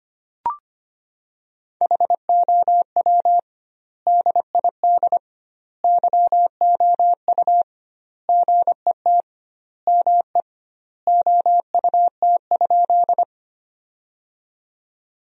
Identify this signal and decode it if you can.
Morse code, heard as a keyed tone, about 25 wpm, 700 Hz. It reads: HOW DID YOU GET ME OUT?
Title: 2x Word-Spacing - Sentences from Top 200 Words - 25wpm